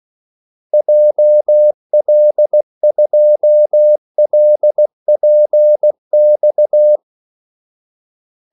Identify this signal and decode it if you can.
Morse code, heard as a keyed tone, about 16 wpm, 600 Hz. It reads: JL2LPX
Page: Kaldesignaler 16 wpm